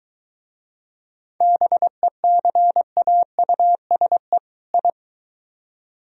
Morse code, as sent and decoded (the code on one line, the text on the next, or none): -... -.-. .- ..- ... / ..
BECAUSE I